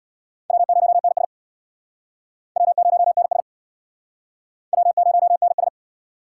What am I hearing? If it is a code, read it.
Morse code, heard as a keyed tone, about 63 wpm, 700 Hz. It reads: W0NS W0NS W0NS